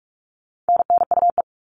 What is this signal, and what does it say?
Morse code, heard as a keyed tone, about 44 wpm, 700 Hz. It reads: NNUE
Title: All Cut Numbers - 4-Digit 45wpm